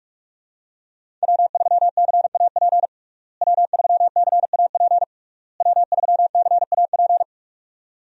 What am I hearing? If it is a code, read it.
Morse code, heard as a keyed tone, about 46 wpm, 700 Hz. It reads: W3CAP W3CAP W3CAP
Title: USA Call Signs - 5 Characters - Repeated 3x - 45wpm